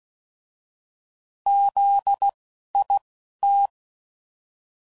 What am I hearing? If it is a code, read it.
Morse code, heard as a keyed tone, about 16 wpm, spaced as for 11 wpm, 800 Hz. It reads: ZIT
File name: Prefixes / By First Letter / Zs